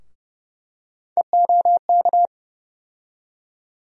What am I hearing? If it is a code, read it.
Morse code, heard as a keyed tone, about 30 wpm, 700 Hz. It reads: EOK
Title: USA Call Sign Postfixes - 3 Letters 30wpm